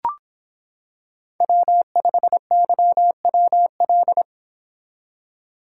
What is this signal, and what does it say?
Morse code, heard as a keyed tone, about 25 wpm, 700 Hz. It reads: W5YWL